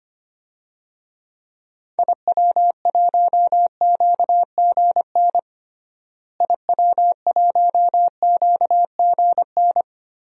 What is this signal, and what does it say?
Morse code, heard as a keyed tone, about 25 wpm, 700 Hz. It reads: IW1QGN IW1QGN
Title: DXCC-100 Call Signs - 6 Characters - Repeated 2x - 25wpm